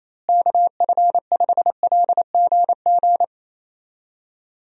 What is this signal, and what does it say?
Morse code, heard as a keyed tone, about 28 wpm, 700 Hz. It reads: KF5LGG
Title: USA Call Signs - 6 Characters - Repeated 2x - 28wpm